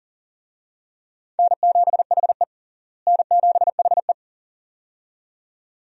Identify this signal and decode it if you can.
Morse code, heard as a keyed tone, about 40 wpm, 700 Hz. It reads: N7HE N7HE
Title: USA Call Signs - 4 Characters - Repeated 2x - 40wpm